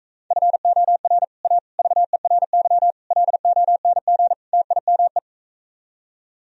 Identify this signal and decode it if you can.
Morse code, heard as a keyed tone, about 45 wpm, 700 Hz. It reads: FOR A VERY LONG TIME